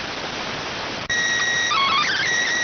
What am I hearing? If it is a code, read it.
SSTV (slow-scan television), PD 290.